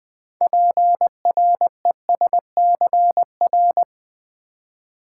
Morse code, heard as a keyed tone, about 20 wpm, 700 Hz. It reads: PRESCR
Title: Build A Word - 6 Letters At A Time - 12-Letter Words - 20wpm